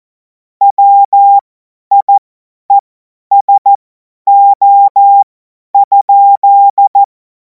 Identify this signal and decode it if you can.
Morse code, heard as a keyed tone, about 14 wpm, spaced as for 10 wpm, 800 Hz. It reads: WIESO?
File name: Morsekurs / Witze / Witz 18. August 2026 50 F70